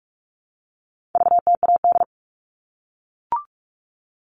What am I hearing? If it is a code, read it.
Morse code, heard as a keyed tone, about 44 wpm, 700 Hz. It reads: VTAD
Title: All Cut Numbers - 4-Digit 45wpm